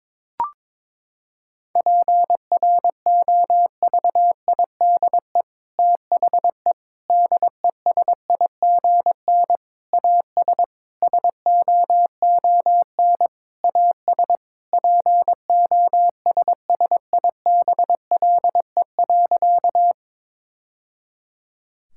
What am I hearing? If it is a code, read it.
Morse code, heard as a keyed tone, about 21 wpm, 700 Hz. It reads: PROVIDE THE DESIGN AS SOON AS POSSIBLE.
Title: Sentences from Top 1000 Words - Review Part 5 - 22wpm